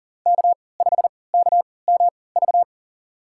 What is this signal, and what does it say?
Morse code, heard as a keyed tone, about 40 wpm, spaced as for 23 wpm, 700 Hz. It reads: K5KMV
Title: USA Call Signs - 3x Character Spacing - 40wpm